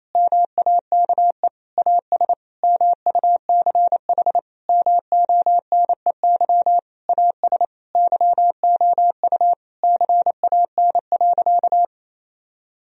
Morse code, heard as a keyed tone, about 28 wpm, 700 Hz. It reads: MAKE AS MUCH MONEY AS YOU CAN.
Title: Sentences from Top 1000 Words - Review Part 3 - 28wpm